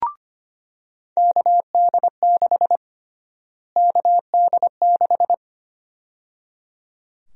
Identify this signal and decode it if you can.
Morse code, heard as a keyed tone, about 25 wpm, 700 Hz. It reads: KD6 KD6
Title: USA Call Sign Prefixes - Repeated 2x - 25wpm